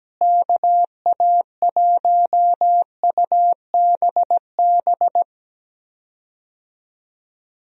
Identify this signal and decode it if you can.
Morse code, heard as a keyed tone, about 17 wpm, 700 Hz. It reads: KA1UBB
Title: USA Call Signs - 6 Characters - Repeated 2x - 17wpm